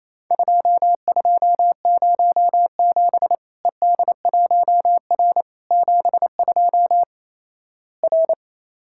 Morse code, heard as a keyed tone, about 28 wpm, 700 Hz. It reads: 2207 ED1R 72 R
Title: Worked All Europe DX Contest - QTC Exchange - 28wpm